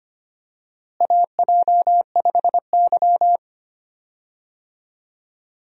Morse code, heard as a keyed tone, about 25 wpm, 700 Hz. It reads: AJ5Y
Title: USA Call Signs - 4 Characters - Repeated 3x - 25wpm